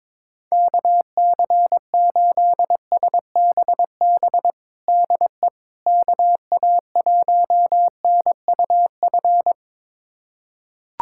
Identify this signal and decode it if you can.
Morse code, heard as a keyed tone, about 22 wpm, 700 Hz. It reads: KC8SBB DE KA1NUF
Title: USA Callsign DE Callsign - 6 Characters - 22wpm